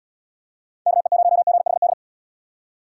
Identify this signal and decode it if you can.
Morse code, heard as a keyed tone, about 70 wpm, 700 Hz. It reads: KE0GHD